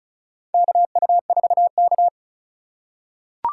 Morse code, heard as a keyed tone, about 36 wpm, 700 Hz. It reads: KU4K
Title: USA Call Signs - 2x Character Spacing - 35wpm